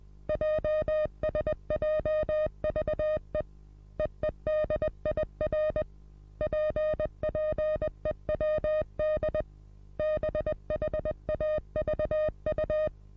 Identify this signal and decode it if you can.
Morse code, heard as a keyed tone, about 20 wpm, 620 Hz. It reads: JSJVE EEDIR PPEWD BHAVU
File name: repos / Morsekurs / DJ4UF Morsen27